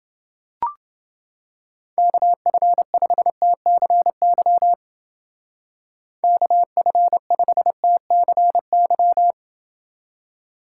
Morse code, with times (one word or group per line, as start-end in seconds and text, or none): 2.0-4.7 KF5TCY
6.2-9.3 KF5TCY